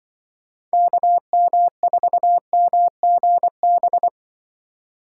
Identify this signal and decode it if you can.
Morse code, heard as a keyed tone, about 25 wpm, 700 Hz. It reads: KM4MGB
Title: Speed Racing - USA 2x3 Call Signs 20wpm